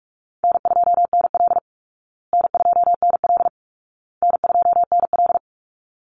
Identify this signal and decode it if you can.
Morse code, heard as a keyed tone, about 44 wpm, 700 Hz. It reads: N2NL N2NL N2NL
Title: Easy Call Signs Repeated 3x - 45wpm